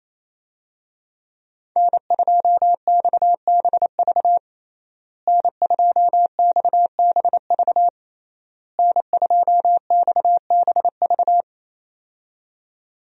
Morse code, heard as a keyed tone, about 28 wpm, 700 Hz. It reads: N2XBV N2XBV N2XBV
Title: USA Call Signs - 5 Characters - Repeated 3x - 28wpm